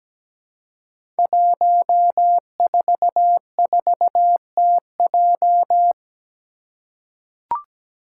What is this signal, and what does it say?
Morse code, heard as a keyed tone, about 17 wpm, 700 Hz. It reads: 144TJ
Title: Five Challenging Letter-Number Combinations 17wpm